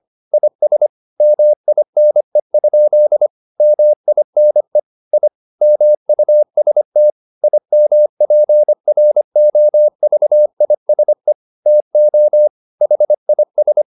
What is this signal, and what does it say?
Morse code, heard as a keyed tone, about 25 wpm, 600 Hz. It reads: IS MINE? MINE I MUST IMPROVISE TO HIS
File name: the old man and the sea part 1 25wpm